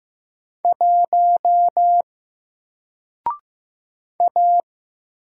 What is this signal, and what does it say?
Morse code, heard as a keyed tone, about 15 wpm, 700 Hz. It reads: 1 A
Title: Single Letter-Number Alt 15wpm